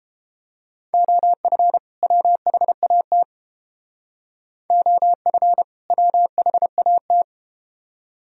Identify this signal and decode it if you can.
Morse code, heard as a keyed tone, about 31 wpm, 700 Hz. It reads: OF WHAT OF WHAT